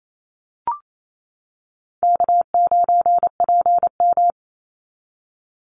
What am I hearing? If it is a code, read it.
Morse code, heard as a keyed tone, about 28 wpm, 700 Hz. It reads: K9PM
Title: NA SOTA Activator Callsigns 28wpm